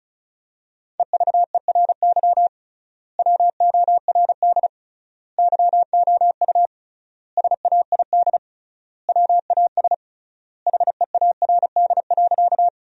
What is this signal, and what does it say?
Morse code, heard as a keyed tone, about 36 wpm, 700 Hz. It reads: EVERY WORD YOU SAID WAS HEARD.
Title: 3x Word-Spacing - Sentences from Top 400 Words - 35wpm